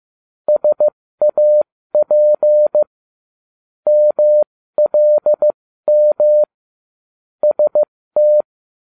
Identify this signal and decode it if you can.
Morse code, heard as a keyed tone, about 15 wpm, spaced as for 13 wpm, 600 Hz. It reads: SAP MLM ST